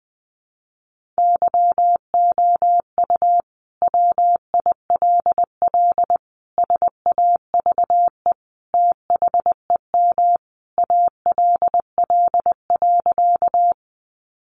Morse code, heard as a keyed tone, about 19 wpm, 700 Hz. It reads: YOU WILL SAVE THEM ALL.
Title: Sentences from Top 1100 Words 20wpm